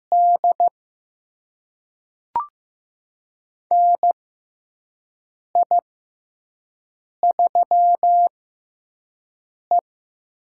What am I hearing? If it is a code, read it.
Morse code, heard as a keyed tone, about 15 wpm, spaced as for 5 wpm, 700 Hz. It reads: D NI3E
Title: USA Call Signs - 6x Character Spacing - 15wpm